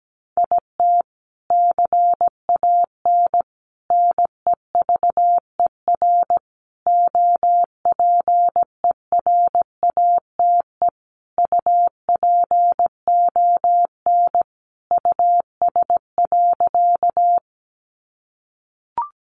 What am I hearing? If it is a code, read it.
Morse code, heard as a keyed tone, about 17 wpm, 700 Hz. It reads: IT CAN NEVER OPERATE UPON US.
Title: Sentences from Top 1500 Words 17wpm